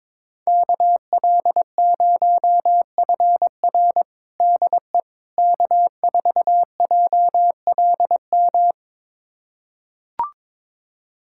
Morse code, heard as a keyed tone, about 22 wpm, 700 Hz. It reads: KL0FR DE K4JLM
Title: USA Callsign DE Callsign - 5 Characters - 22wpm